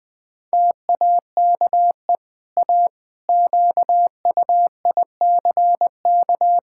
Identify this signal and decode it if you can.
Morse code, heard as a keyed tone, about 20 wpm, 700 Hz. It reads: TAKE A QUICK
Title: Easy Sentences with Hints - 20wpm